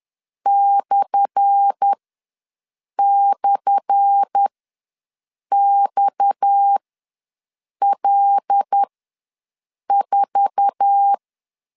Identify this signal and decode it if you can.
Morse code, heard as a keyed tone, about 10 wpm, spaced as for 6 wpm, 800 Hz. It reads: //XL4
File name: LEZIONI / 10 SIGNS / 10A SLASH ws 10wpm 4121